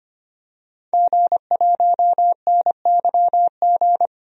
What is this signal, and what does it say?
Morse code, heard as a keyed tone, about 25 wpm, 700 Hz. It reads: G1NYG
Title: DXCC-100 Call Signs - 5 Characters - 25wpm